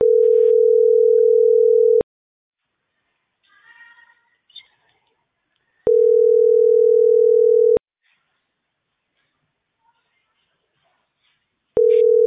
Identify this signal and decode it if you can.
Ringback tone (North America), 440 + 480 Hz.